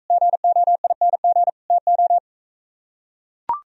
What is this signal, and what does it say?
Morse code, heard as a keyed tone, about 44 wpm, 700 Hz. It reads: GOING TO